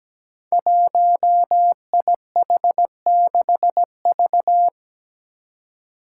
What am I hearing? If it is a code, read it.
Morse code, heard as a keyed tone, about 17 wpm, 700 Hz. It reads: 1IH6V
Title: Five Challenging Letter-Number Combinations 17wpm